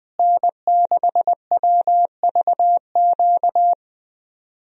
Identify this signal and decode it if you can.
Morse code, heard as a keyed tone, about 20 wpm, 700 Hz. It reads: N6WVQ
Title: USA Call Signs - 5 Characters - Repeated 3x - 20wpm